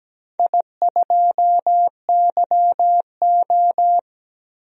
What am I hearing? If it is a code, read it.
Morse code, heard as a keyed tone, about 17 wpm, 700 Hz. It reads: I2YO
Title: DXCC-100 Call Signs - 4 Characters - 17wpm